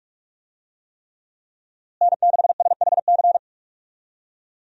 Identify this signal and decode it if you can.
Morse code, heard as a keyed tone, about 42 wpm, 700 Hz. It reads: N6SHC
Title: USA Call Signs - 5 Characters - Repeated 3x - 45wpm